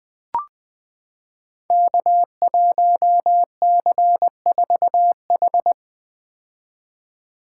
Morse code, heard as a keyed tone, about 20 wpm, 700 Hz. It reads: K1C4H
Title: Code Groups - 5 Letter:Numbers - 20wpm